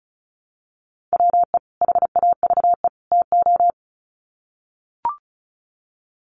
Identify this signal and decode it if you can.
Morse code, heard as a keyed tone, about 34 wpm, 700 Hz. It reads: WE HAVE TO
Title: Sets of 3 Words 35wpm